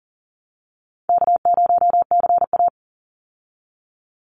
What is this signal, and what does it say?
Morse code, heard as a keyed tone, about 40 wpm, 700 Hz. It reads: K0CA
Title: Easy Call Signs Repeated 2x - 40wpm